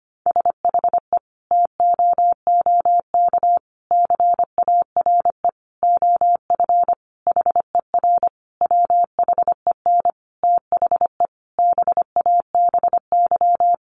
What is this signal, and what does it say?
Morse code, heard as a keyed tone, about 25 wpm, 700 Hz. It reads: SHE TOOK CARE OF HER WHEN THE BABY
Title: Sentences from Top 1300 Words 25wpm